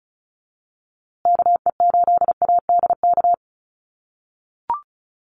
Morse code, heard as a keyed tone, about 34 wpm, 700 Hz. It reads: KE8ADK
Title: USA Call Signs Repeated 2x 35wpm